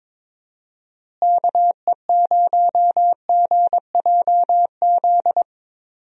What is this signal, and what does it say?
Morse code, heard as a keyed tone, about 22 wpm, 700 Hz. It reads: KE0GJZ
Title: USA Call Signs - 6 Characters - Repeated 2x - 22wpm